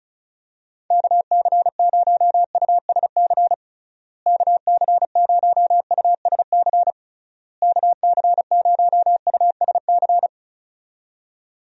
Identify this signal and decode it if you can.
Morse code, heard as a keyed tone, about 34 wpm, 700 Hz. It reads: KC0USC KC0USC KC0USC
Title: USA Call Signs - 6 Characters - Repeated 3x - 35wpm